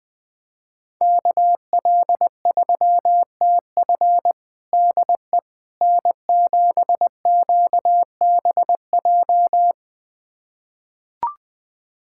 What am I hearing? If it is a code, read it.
Morse code, heard as a keyed tone, about 20 wpm, 700 Hz. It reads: KL3TF DE N7QBJ
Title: USA Callsign DE Callsign - 5 Characters - 20wpm